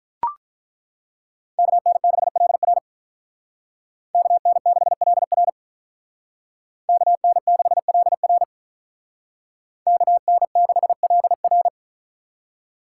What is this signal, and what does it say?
Morse code, heard as a keyed tone, about 43 wpm, 700 Hz. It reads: KN6LR KN6LR KN6LR KN6LR
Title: Speed Racing - USA 2x2 Call Signs 35wpm